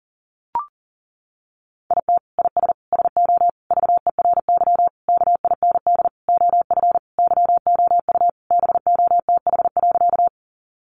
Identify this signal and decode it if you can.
Morse code, heard as a keyed tone, about 40 wpm, 700 Hz. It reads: IT IS SO VERY KIND OF YOU BOTH.
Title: Sentences from Top 300 Words 40wpm